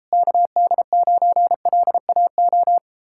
Morse code, heard as a keyed tone, about 38 wpm, 700 Hz. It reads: KD9LAO